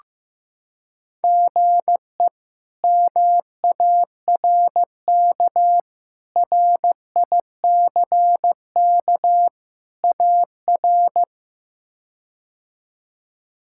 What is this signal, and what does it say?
Morse code, heard as a keyed tone, about 15 wpm, 700 Hz. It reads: GE MARK RICK AR
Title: K1USN SST QSO Practice - 15wpm